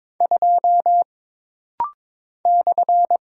Morse code, heard as a keyed tone, about 22 wpm, 700 Hz. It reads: 2 /
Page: This rapid-fire timing is geared to push your ICR (Instant Character Recognition) skills.